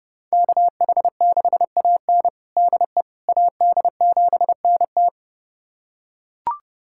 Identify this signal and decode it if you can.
Morse code, heard as a keyed tone, about 32 wpm, 700 Hz. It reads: KH6AN DE AD7NT
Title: USA Callsign DE Callsign 30wpm